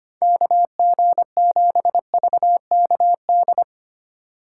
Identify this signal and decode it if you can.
Morse code, heard as a keyed tone, about 25 wpm, 700 Hz. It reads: KG7VKD
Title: USA Call Signs - 6 Characters - 3x Character Spacing - 25wpm